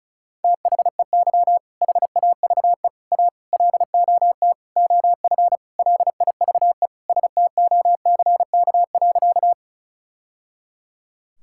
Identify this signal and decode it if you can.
Morse code, heard as a keyed tone, about 36 wpm, 700 Hz. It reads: THEY HAVE A LOT OF LIVE STOCK.
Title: Sentences from Top 1000 Words - Review Part 5 - 35wpm